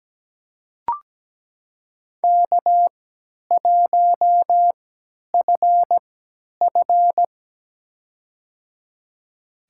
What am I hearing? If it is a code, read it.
Morse code, heard as a keyed tone, about 17 wpm, spaced as for 10 wpm, 700 Hz. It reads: K1FF
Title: USA Call Signs - 3x Character Spacing - 17wpm